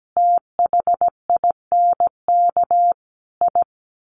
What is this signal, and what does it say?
Morse code, heard as a keyed tone, about 17 wpm, 700 Hz. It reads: THINK I
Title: Sets of 5 Words Encore 17wpm